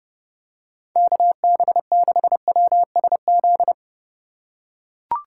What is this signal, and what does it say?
Morse code, heard as a keyed tone, about 30 wpm, 700 Hz. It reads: KB6WSZ
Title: USA Call Signs - 6 Characters - 2x Character Spacing - 30wpm